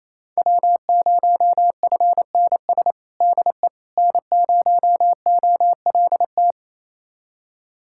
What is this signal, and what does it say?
Morse code, heard as a keyed tone, about 28 wpm, 700 Hz. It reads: W0FNS DE N0OLT
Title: USA Callsign DE Callsign - 5 Characters - 28wpm